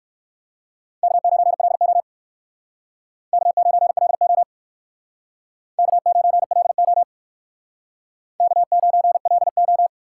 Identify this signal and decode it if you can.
Morse code, heard as a keyed tone, about 56 wpm, 700 Hz. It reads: K9LO K9LO K9LO K9LO